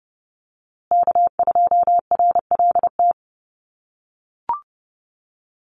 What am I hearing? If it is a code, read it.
Morse code, heard as a keyed tone, about 30 wpm, 700 Hz. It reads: K2RLT
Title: USA Call Signs Repeated 2x 30wpm